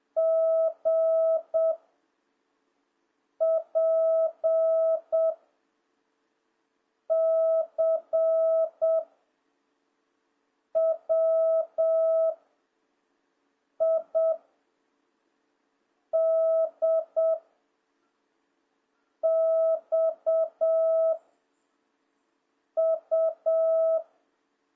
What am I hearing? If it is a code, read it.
Morse code, heard as a keyed tone, about 7 wpm, 660 Hz. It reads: G P C W I D X U